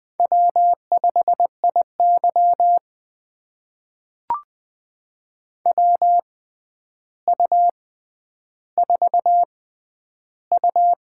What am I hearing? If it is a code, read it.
Morse code, heard as a keyed tone, about 20 wpm, spaced as for 10 wpm, 700 Hz. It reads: W5IY WU4U
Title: USA Call Signs - 4 Characters - 6x Character Spacing - 20wpm